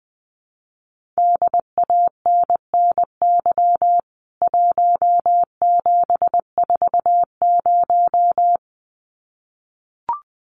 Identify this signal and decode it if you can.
Morse code, heard as a keyed tone, about 20 wpm, 700 Hz. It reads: DANNY 1740
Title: CWT Exchange 20wpm